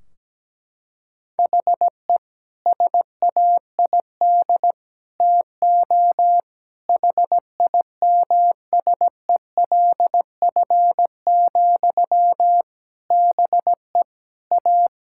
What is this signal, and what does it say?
Morse code, heard as a keyed tone, about 17 wpm, 700 Hz. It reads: HE SAID TO HIMSELF, BE A
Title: Sentences from Top 1600 Words 17wpm